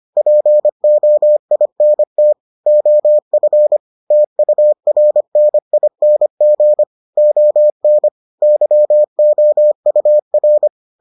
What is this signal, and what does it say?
Morse code, heard as a keyed tone, about 25 wpm, 600 Hz. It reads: POINT OF TURNING ON YOUR